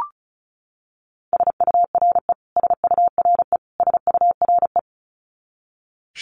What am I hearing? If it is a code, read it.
Morse code, heard as a keyed tone, about 34 wpm, 700 Hz. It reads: SURE SURE SURE
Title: Top 300 Words Repeated 35wpm